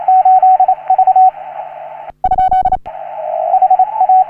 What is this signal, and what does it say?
Morse code, heard as a keyed tone, about 39 wpm, 750 Hz, with radio noise. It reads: 8 V ? H A
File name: CW / HA3OU RHOBCW REC